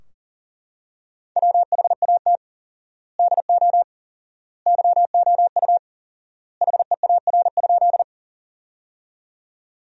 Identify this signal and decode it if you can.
Morse code, heard as a keyed tone, about 40 wpm, 700 Hz. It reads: WHAT DO YOU HEAR?
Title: 4x Word-Spacing - Sentences from Top 300 Words - 40wpm